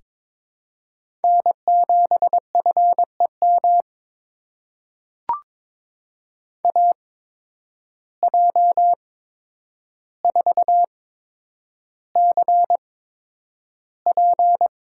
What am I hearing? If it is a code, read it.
Morse code, heard as a keyed tone, about 22 wpm, spaced as for 9 wpm, 700 Hz. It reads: N7FEM AJ4CP
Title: USA Call Signs - 5 Characters - 8x Character Spacing - 22wpm